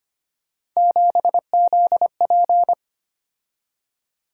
Morse code, heard as a keyed tone, about 25 wpm, 700 Hz. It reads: 7ZP
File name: Three Letter-Number Combinations - 25wpm